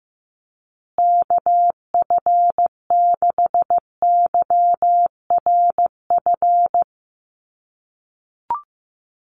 Morse code, heard as a keyed tone, about 15 wpm, 700 Hz. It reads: KF6YRF
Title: USA Call Signs Repeated 2x 15wpm